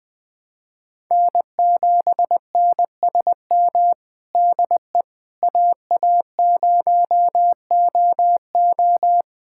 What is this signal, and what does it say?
Morse code, heard as a keyed tone, about 20 wpm, 700 Hz. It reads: N7NSM DE AA0OO
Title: USA Callsign DE Callsign - 5 Characters - 20wpm